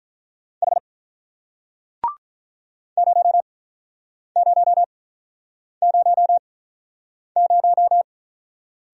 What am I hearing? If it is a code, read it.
Morse code, heard as a keyed tone, about 41 wpm, spaced as for 8 wpm, 700 Hz. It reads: H 0000
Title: Speed-Racing Single Letter-Number 35wpm